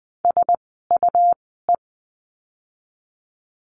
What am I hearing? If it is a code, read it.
Morse code, heard as a keyed tone, about 20 wpm, spaced as for 14 wpm, 700 Hz. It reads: SUE
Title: 101 Lesson 17f - Letter u - Words 20wpm